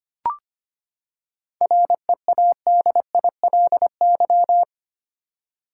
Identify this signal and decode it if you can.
Morse code, heard as a keyed tone, about 25 wpm, 700 Hz. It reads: READILY
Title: Seven Letter Words - 25wpm